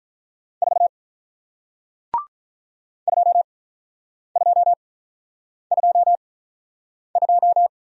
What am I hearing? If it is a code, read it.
Morse code, heard as a keyed tone, about 48 wpm, spaced as for 9 wpm, 690 Hz. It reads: 4 2222